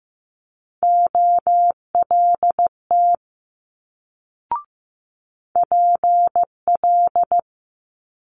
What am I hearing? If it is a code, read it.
Morse code, heard as a keyed tone, about 15 wpm, 700 Hz. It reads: OLT PL
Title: Multi-letter Phonemes Quick Fire 15wpm